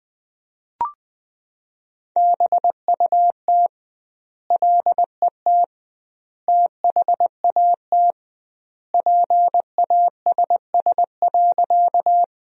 Morse code, heard as a keyed tone, about 21 wpm, 700 Hz. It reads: BUT LET THAT PASS.
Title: 2x Word-Spacing - Sentences from Top 400 Words - 20wpm